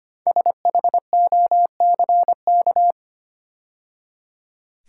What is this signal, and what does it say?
Morse code, heard as a keyed tone, about 25 wpm, 700 Hz. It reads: SHOCK
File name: 5 Letter Words - First 3 Letters Given - 25wpm